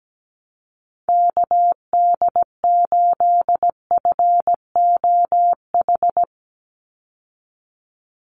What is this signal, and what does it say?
Morse code, heard as a keyed tone, about 17 wpm, 700 Hz. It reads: KD8FOH
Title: USA Call Signs 17wpm